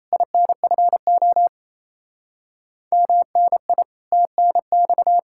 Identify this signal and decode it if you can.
Morse code, heard as a keyed tone, about 38 wpm, 700 Hz. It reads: INFO MNI TNX